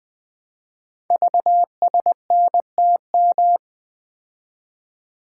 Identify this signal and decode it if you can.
Morse code, heard as a keyed tone, about 20 wpm, 700 Hz. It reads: VSNTM
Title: Code Groups - 5 Letter:Numbers - 20wpm